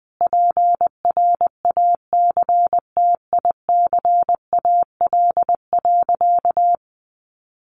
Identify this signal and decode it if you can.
Morse code, heard as a keyed tone, about 20 wpm, 700 Hz. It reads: PRACTICAL.
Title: Sentences from Top 1400 Words 20wpm